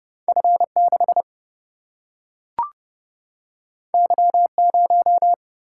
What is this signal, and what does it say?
Morse code, heard as a keyed tone, about 31 wpm, 700 Hz. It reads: F6 Y0
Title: Single-Letter Single-Number 30wpm